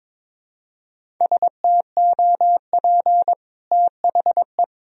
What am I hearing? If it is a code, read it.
Morse code, heard as a keyed tone, about 21 wpm, 700 Hz. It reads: STOP THE
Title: Sentences from Top 1000 Words - Review Part 1 - 22wpm